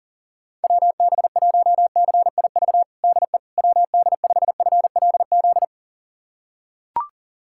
Morse code, heard as a keyed tone, about 40 wpm, 700 Hz. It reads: WB1CIV DE WD5FLZ
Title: USA Callsign DE Callsign - 6 Characters - 40wpm